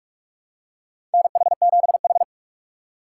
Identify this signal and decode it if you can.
Morse code, heard as a keyed tone, about 41 wpm, 690 Hz. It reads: NH7H
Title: USA Call Signs - 4 Characters - 3x Character Spacing - 45wpm